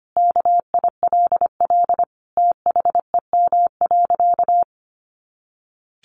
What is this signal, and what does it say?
Morse code, heard as a keyed tone, about 25 wpm, 700 Hz. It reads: KILL THEM.
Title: Easy Phrases 25wpm